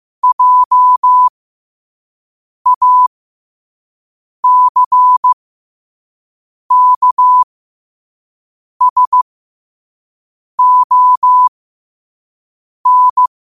Morse code, heard as a keyed tone, about 15 wpm, spaced as for 5 wpm, 1000 Hz. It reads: JACKSON
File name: qotdfiles / 5wpm